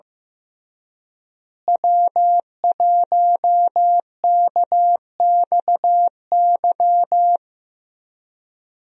Morse code, heard as a keyed tone, about 15 wpm, 700 Hz. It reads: W1KXY
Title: USA Call Signs - 5 Characters - Repeated 3x - 15wpm